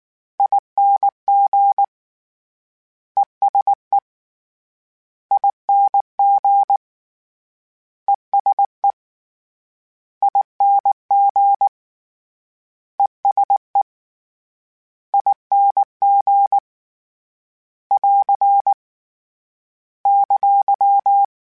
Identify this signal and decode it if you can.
Morse code, heard as a keyed tone, about 19 wpm, 800 Hz. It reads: ING ESE ING ESE ING ESE ING <AR> !
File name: General / General-mp3s / ESE ING @18